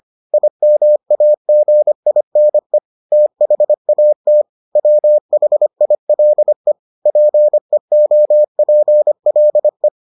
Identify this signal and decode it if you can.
Morse code, heard as a keyed tone, about 25 wpm, 600 Hz. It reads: IMAGINE THAT WHILE PEOPLE